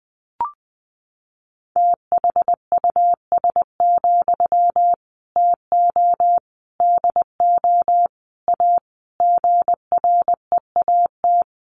Morse code, heard as a keyed tone, about 20 wpm, 700 Hz. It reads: THUS, TO DO A GREAT
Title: Sentences from Top 900 Words 20wpm